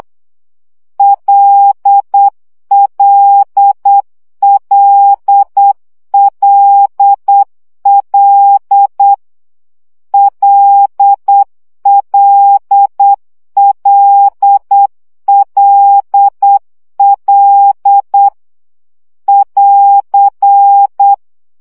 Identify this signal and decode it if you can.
Morse code, heard as a keyed tone, about 8 wpm, 800 Hz. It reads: LLLLL LLLLL <AR>
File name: CW gyakorlas l 7 baud